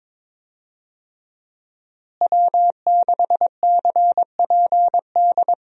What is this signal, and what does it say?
Morse code, heard as a keyed tone, about 22 wpm, 700 Hz. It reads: W6CPD
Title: USA Call Signs - 5 Characters - Repeated 3x - 22wpm